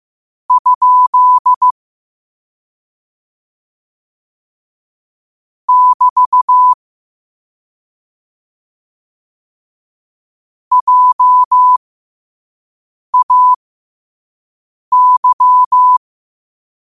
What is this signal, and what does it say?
Morse code, heard as a keyed tone, about 15 wpm, spaced as for 5 wpm, 1000 Hz. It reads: ? <BT> JAY
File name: qotdfiles / 5wpm